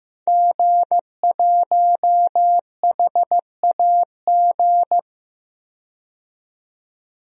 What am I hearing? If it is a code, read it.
Morse code, heard as a keyed tone, about 15 wpm, 700 Hz. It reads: G1HAG
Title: DXCC-100 Call Signs - 5 Characters - 15wpm